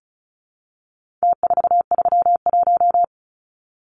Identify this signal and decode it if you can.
Morse code, heard as a keyed tone, about 34 wpm, 700 Hz. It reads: T431
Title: Numbers - Contest Style - 4-Digit 35wpm